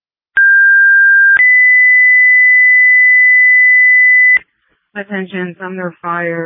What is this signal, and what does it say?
Two-tone page, heard as a one-second tone, then a three-second tone, about 1600 then 2000 Hz, with speech. It is heard from 0.4 to 4.4 s.